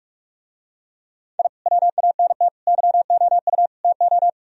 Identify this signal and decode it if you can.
Morse code, heard as a keyed tone, about 49 wpm, 700 Hz. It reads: I WANT YOU TO